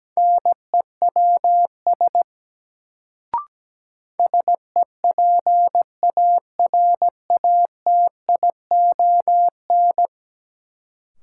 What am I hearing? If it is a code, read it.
Morse code, heard as a keyed tone, about 17 wpm, 700 Hz. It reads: NEWS SEPARATION
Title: Sets of 5 Words - New Word Drill - 17wpm